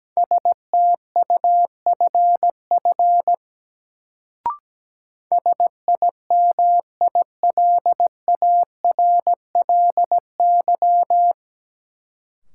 Morse code, heard as a keyed tone, about 17 wpm, 700 Hz. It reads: STUFF SIMILARLY
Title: Sentences from Top 1400 Words - New Word Drill - 17wpm